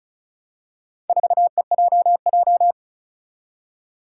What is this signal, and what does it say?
Morse code, heard as a keyed tone, about 36 wpm, 700 Hz. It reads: 4EJJ
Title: Four Challenging Letter-Number Combinations 35wpm